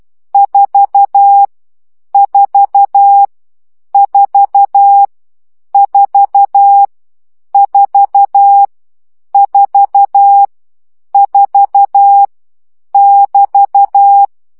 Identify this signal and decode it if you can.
Morse code, heard as a keyed tone, about 12 wpm, 800 Hz. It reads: 4 4 4 4 4 4 4 <BT>
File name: CW gyakorlas 4 10 baud